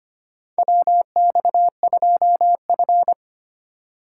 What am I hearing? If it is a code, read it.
Morse code, heard as a keyed tone, about 25 wpm, 700 Hz. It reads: WX2F
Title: USA Call Signs - 4 Characters - 4x Character Spacing - 25wpm